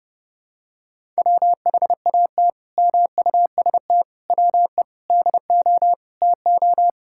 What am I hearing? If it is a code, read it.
Morse code, heard as a keyed tone, about 30 wpm, 700 Hz. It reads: WHAT MUST WE DO TO
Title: Sentences from Top 1000 Words - Review Part 6 - 30wpm